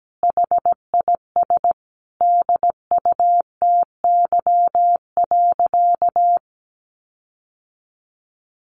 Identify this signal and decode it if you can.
Morse code, heard as a keyed tone, about 17 wpm, 700 Hz. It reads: HIS DUTY.
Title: Sentences from Top 1500 Words 17wpm